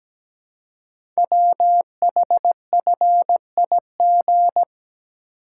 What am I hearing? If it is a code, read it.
Morse code, heard as a keyed tone, about 17 wpm, 700 Hz. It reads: WHFIG
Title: Code Groups - 5 Letters - 17wpm